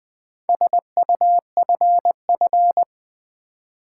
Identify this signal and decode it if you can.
Morse code, heard as a keyed tone, about 20 wpm, 690 Hz. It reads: SUFF
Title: Build A Word - 2 Letters At A Time - 6-Letter Words - 20wpm